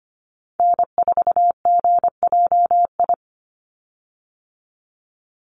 Morse code, heard as a keyed tone, about 25 wpm, 700 Hz. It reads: N4GJI
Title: USA Call Signs 25wpm